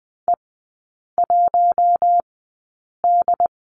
Morse code, heard as a keyed tone, about 20 wpm, spaced as for 8 wpm, 700 Hz. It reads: E1D
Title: 074 Lesson 13e - Letter L - Three Characters 20wpm